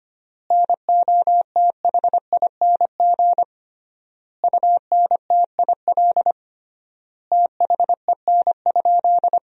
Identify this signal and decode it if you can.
Morse code, heard as a keyed tone, about 25 wpm, 700 Hz. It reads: NOTHING UNTIL THEN?
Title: 3x Word-Spacing - Sentences from Top 500 Words - 25wpm